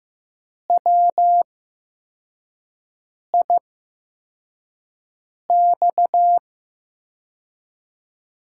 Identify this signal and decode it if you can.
Morse code, heard as a keyed tone, about 15 wpm, spaced as for 4 wpm, 700 Hz. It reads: WIX
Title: USA Call Signs - 8x Character Spacing - 15wpm